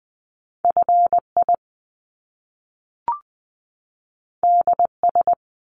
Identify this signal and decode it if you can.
Morse code, heard as a keyed tone, about 20 wpm, 700 Hz. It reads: FI DS
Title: Top Two Letter Combinations in Words 20wpm